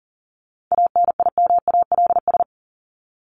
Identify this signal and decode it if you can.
Morse code, heard as a keyed tone, about 40 wpm, 700 Hz. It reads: ANIMALS
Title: ES:S Suffix - Top 100 Words with Base Words 40wpm